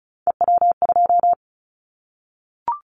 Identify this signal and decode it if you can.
Morse code, heard as a keyed tone, about 34 wpm, 700 Hz. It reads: EW2
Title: Callsign Prefixes for Top 50 DXCC - 35wpm